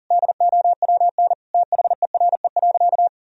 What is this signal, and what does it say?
Morse code, heard as a keyed tone, about 40 wpm, 700 Hz. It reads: DOWN THERE.
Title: Sentences from Top 1000 Words - Review Part 4 - 40wpm